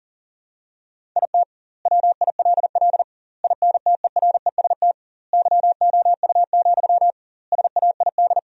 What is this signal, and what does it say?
Morse code, heard as a keyed tone, about 40 wpm, 700 Hz. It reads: IT WILL INTEREST YOU, SAID
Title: 2x Word-Spacing - Sentences from Top 400 Words - 40wpm